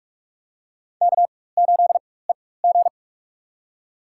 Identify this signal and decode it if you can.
Morse code, heard as a keyed tone, about 46 wpm, spaced as for 21 wpm, 700 Hz. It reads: K8EG
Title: USA Call Signs - 4 Characters - 4x Character Spacing - 45wpm